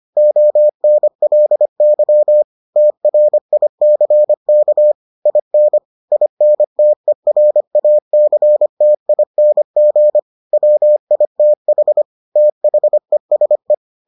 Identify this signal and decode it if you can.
Morse code, heard as a keyed tone, about 25 wpm, 600 Hz. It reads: ONLY TRICK IN INTERACTING WITH THESE